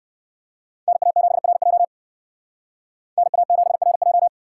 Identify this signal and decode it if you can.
Morse code, heard as a keyed tone, about 52 wpm, 700 Hz. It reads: NA7RJ NA7RJ
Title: Speed Racing - USA 2x2 Call Signs 45wpm